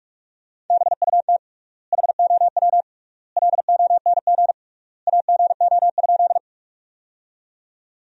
Morse code, heard as a keyed tone, about 46 wpm, 700 Hz. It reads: BUT HOW LONG AGO?
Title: 3x Word-Spacing - Sentences from Top 500 Words - 45wpm